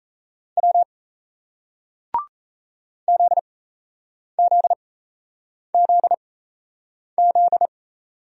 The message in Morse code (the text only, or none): W ZZZZ